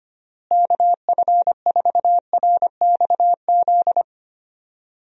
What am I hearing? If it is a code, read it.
Morse code, heard as a keyed tone, about 25 wpm, 700 Hz. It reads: KF4RXZ
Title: USA Call Signs - 6 Characters - Repeated 3x - 25wpm